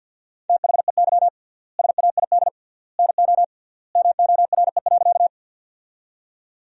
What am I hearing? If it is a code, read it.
Morse code, heard as a keyed tone, about 50 wpm, 700 Hz. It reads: THEY SAID NO MORE.
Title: 3x Word-Spacing - Sentences from Top 100 Words - 50wpm